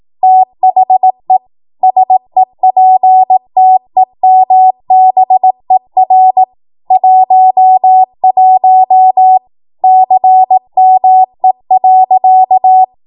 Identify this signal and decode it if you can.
Morse code, heard as a keyed tone, about 18 wpm, 750 Hz. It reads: THE SEPTEMBER 11 CME.
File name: Morse / 18WPM